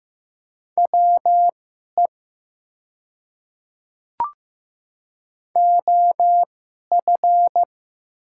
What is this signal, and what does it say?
Morse code, heard as a keyed tone, about 15 wpm, spaced as for 11 wpm, 700 Hz. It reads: WE OF
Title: 2x Character Spacing - Top 100 Words - 15wpm